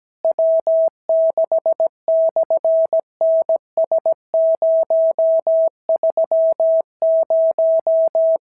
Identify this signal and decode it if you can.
Morse code, heard as a keyed tone, about 17 wpm, 650 Hz. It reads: W6/NS030
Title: SOTA QSO Practice 17wpm